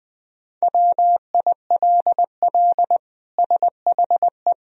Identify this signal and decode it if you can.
Morse code, heard as a keyed tone, about 21 wpm, 700 Hz. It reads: WILL SHE
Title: Sentences from Top 1000 Words - Review Part 2 - 20wpm